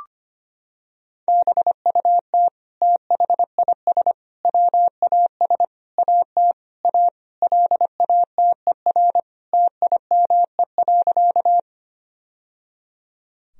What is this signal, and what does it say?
Morse code, heard as a keyed tone, about 25 wpm, 700 Hz. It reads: BUT THIS WAS AT A LATER TIME.
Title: Sentences from Top 1000 Words - Review Part 4 - 25wpm